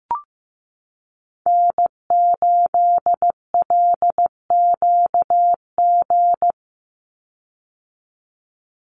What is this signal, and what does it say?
Morse code, heard as a keyed tone, about 15 wpm, 700 Hz. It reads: N8LQG
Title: USA Call Signs 15wpm